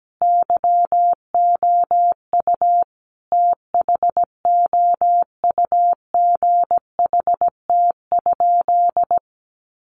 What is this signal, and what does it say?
Morse code, heard as a keyed tone, about 17 wpm, 700 Hz. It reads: YOU THOUGHT?
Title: Sentences from Top 300 Words 17wpm